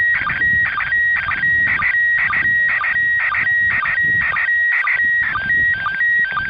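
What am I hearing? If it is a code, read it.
SSTV (slow-scan television), PD 120.
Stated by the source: SSTV transmission from the International Space Station.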